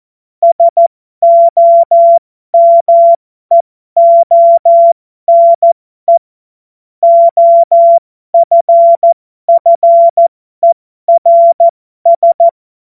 Morse code, heard as a keyed tone, about 14 wpm, spaced as for 12 wpm, 680 Hz. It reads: SOMEONE OFFERS
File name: Morsekurs / Witze